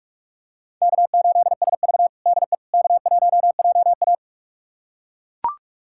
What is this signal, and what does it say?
Morse code, heard as a keyed tone, about 42 wpm, 700 Hz. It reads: K8SV DE K1JA
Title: USA Callsign DE Callsign - 4 Characters - 45wpm